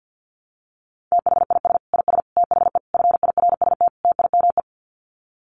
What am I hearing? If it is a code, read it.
Morse code, heard as a keyed tone, about 50 wpm, 700 Hz. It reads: THIS IS THE FIRST TIME
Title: Sets of 5 Words 50wpm